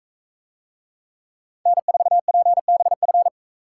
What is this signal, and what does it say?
Morse code, heard as a keyed tone, about 44 wpm, 700 Hz. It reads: N4PBF